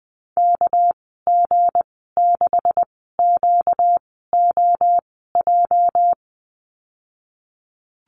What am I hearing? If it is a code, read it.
Morse code, heard as a keyed tone, about 20 wpm, spaced as for 14 wpm, 700 Hz. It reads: KG6QOJ
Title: 238 Lesson 38g - Letter q - Call Signs 20wpm